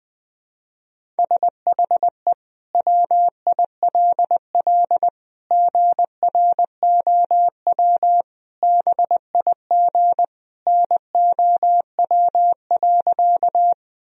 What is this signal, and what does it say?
Morse code, heard as a keyed tone, about 21 wpm, 700 Hz. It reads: SHE WILL GROW BIG NOW.
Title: Sentences from Top 1000 Words - Review Part 1 - 20wpm